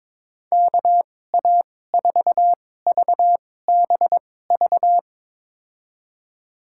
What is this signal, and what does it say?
Morse code, heard as a keyed tone, about 22 wpm, spaced as for 16 wpm, 700 Hz. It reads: KA4VBV
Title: USA Call Signs - 6 Characters - 2x Character Spacing - 22wpm